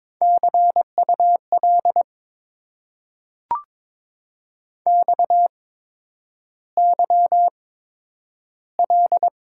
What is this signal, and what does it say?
Morse code, heard as a keyed tone, about 22 wpm, spaced as for 9 wpm, 700 Hz. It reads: CUL XYL
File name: Top 90 QSO Elements - 8x Character Spacing - 22wpm